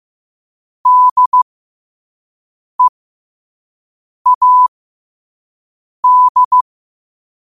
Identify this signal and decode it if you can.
Morse code, heard as a keyed tone, about 15 wpm, spaced as for 5 wpm, 1000 Hz. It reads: DEAD